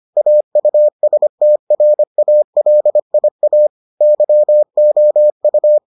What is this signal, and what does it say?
Morse code, heard as a keyed tone, about 25 wpm, 600 Hz. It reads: AUSTRALIA YOU